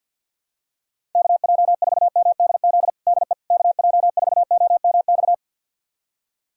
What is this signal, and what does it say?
Morse code, heard as a keyed tone, about 51 wpm, 700 Hz. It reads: KJ4MDZ DE KJ4OMX
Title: USA Callsign DE Callsign - 6 Characters - 50wpm